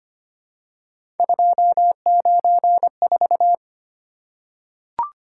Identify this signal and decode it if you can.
Morse code, heard as a keyed tone, about 25 wpm, 700 Hz. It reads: 294
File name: Three Letter-Number Combinations - 25wpm